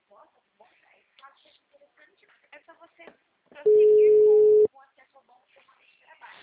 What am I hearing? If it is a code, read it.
Ringback tone (Europe, 425 Hz), with speech.